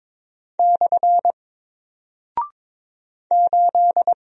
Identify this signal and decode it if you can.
Morse code, heard as a keyed tone, about 22 wpm, 700 Hz. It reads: / 8
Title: Single Character (Letter:Number:Punctuation:Prosign) Alt 22wpm